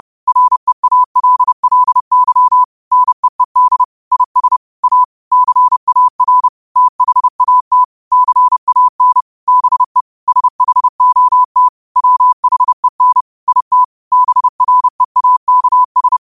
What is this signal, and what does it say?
Morse code, heard as a keyed tone, about 30 wpm, 1000 Hz. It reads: REALLY NEED IS A CAR THAT CAN BE SHOT WHEN IT BREAKS
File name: qotdfiles / 30wpm